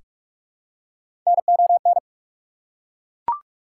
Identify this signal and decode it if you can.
Morse code, heard as a keyed tone, about 42 wpm, 700 Hz. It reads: NON
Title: Prefixes and Suffixes 45wpm